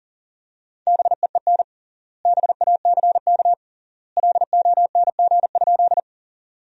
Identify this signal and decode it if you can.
Morse code, heard as a keyed tone, about 40 wpm, 700 Hz. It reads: BEEN BACK LONG?
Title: 3x Word-Spacing - Sentences from Top 200 Words - 40wpm